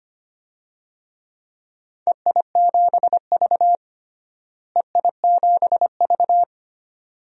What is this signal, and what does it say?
Morse code, heard as a keyed tone, about 25 wpm, 700 Hz. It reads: EI7V EI7V
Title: DXCC-100 Call Signs - 4 Characters - Repeated 2x - 25wpm